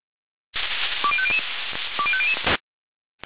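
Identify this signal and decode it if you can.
Five-tone selective call (ZVEI), heard as quick runs of five tones, tones of about 70 ms. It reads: 20500 20500
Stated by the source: ZVEI five-tone selective call 20414 received on SDR.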